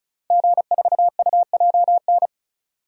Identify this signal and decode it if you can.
Morse code, heard as a keyed tone, about 36 wpm, 700 Hz. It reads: G4UJN
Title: DXCC-100 Call Signs - 5 Characters - 35wpm